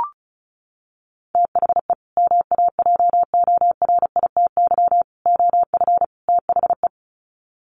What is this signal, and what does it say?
Morse code, heard as a keyed tone, about 34 wpm, 700 Hz. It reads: THE MAJORITY OF THE
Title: Sets of 4 Words 35wpm